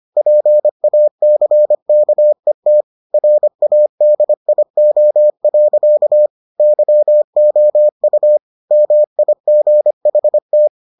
Morse code, heard as a keyed tone, about 25 wpm, 600 Hz. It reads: PACKET RADIO. YOU MIGHT